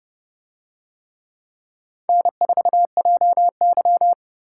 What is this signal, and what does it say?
Morse code, heard as a keyed tone, about 29 wpm, 700 Hz. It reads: N4JY